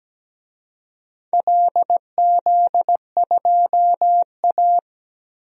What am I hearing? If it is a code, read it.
Morse code, heard as a keyed tone, about 17 wpm, 700 Hz. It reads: LZ2A
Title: DXCC-100 Call Signs - 4 Characters - 17wpm